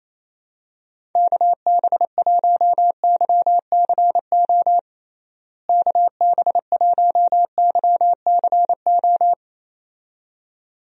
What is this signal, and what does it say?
Morse code, heard as a keyed tone, about 28 wpm, 700 Hz. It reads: KB1YCO KB1YCO
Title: USA Call Signs - 6 Characters - Repeated 2x - 28wpm